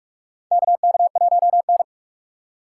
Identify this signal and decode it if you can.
Morse code, heard as a keyed tone, about 43 wpm, 700 Hz. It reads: KK1N
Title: USA Call Signs - 4 Characters - 2x Character Spacing - 45wpm